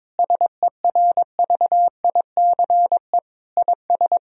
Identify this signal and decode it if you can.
Morse code, heard as a keyed tone, about 22 wpm, 700 Hz. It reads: SERVICE IS
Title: Sentences from Top 1000 Words - Review Part 4 - 22wpm